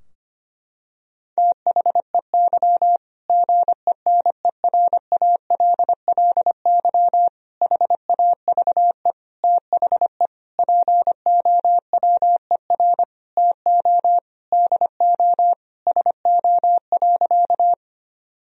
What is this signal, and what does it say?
Morse code, heard as a keyed tone, about 25 wpm, 700 Hz. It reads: THEY GENERALLY HAVE THE POWER TO DO SO.
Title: Sentences from Top 1000 Words - Review Part 4 - 25wpm